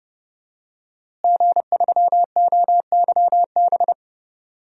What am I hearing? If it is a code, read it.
Morse code, heard as a keyed tone, about 31 wpm, 700 Hz. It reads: G3OYB
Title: DXCC-100 Call Signs - 5 Characters - Repeated 2x - 30wpm